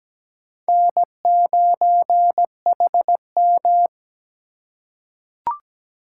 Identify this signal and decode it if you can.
Morse code, heard as a keyed tone, about 17 wpm, 700 Hz. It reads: N9HM
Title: USA Call Signs - 4 Characters - 3x Character Spacing - 17wpm